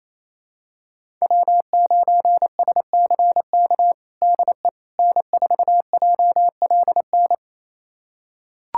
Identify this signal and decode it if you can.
Morse code, heard as a keyed tone, about 28 wpm, 700 Hz. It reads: W9SCK DE N4JLN
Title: USA Callsign DE Callsign - 5 Characters - 28wpm